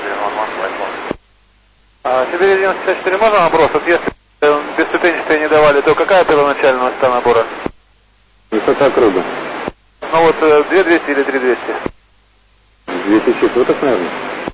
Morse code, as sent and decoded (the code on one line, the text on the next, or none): none